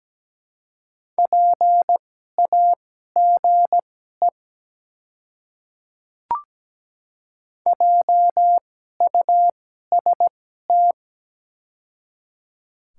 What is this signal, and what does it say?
Morse code, heard as a keyed tone, about 17 wpm, spaced as for 12 wpm, 690 Hz. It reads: PAGE JUST
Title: 2x Character Spacing - Top 300 Words - 17wpm